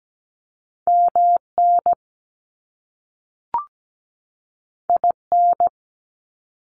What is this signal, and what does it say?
Morse code, heard as a keyed tone, about 17 wpm, 700 Hz. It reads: MN IN
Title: US States 17wpm